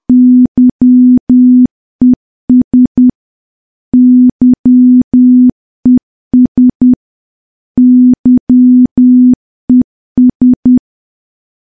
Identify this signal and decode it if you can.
Morse code, heard as a keyed tone, about 10 wpm, 250 Hz. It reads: YES YES YES
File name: morse / morse-training-10WPM